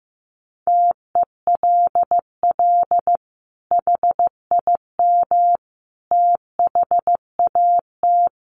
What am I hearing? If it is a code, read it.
Morse code, heard as a keyed tone, about 15 wpm, 700 Hz. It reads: TELL HIM THAT
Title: Sentences from Top 1500 Words 15wpm